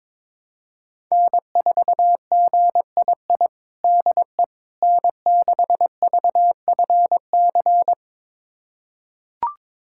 Morse code, heard as a keyed tone, about 22 wpm, 700 Hz. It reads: N4GII DE N6VFC
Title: USA Callsign DE Callsign - 5 Characters - 22wpm